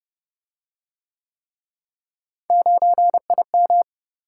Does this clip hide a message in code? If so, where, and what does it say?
Yes: Morse code from 2.5 to 3.8 s, heard as a keyed tone, about 29 wpm, 700 Hz. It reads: 9IM